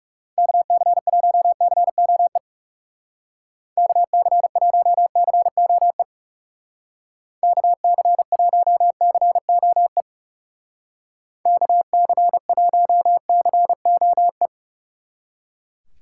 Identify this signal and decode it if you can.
Morse code, heard as a keyed tone, about 37 wpm, 700 Hz. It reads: KC1COE KC1COE KC1COE KC1COE
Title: Speed Racing - USA Call Signs 30wpm